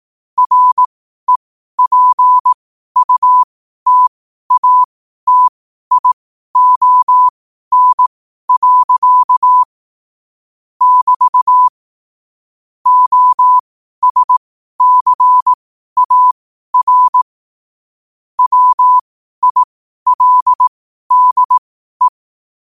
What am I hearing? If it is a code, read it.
Morse code, heard as a keyed tone, about 18 wpm, spaced as for 12 wpm, 1000 Hz. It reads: REPUTATION. <BT> OSCAR WILDE
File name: qotdfiles / 12wpm